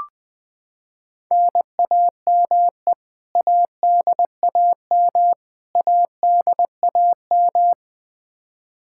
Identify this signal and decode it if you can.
Morse code, heard as a keyed tone, about 20 wpm, 700 Hz. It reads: NAME ADAM ADAM
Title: K1USN SST QSO Practice 20wpm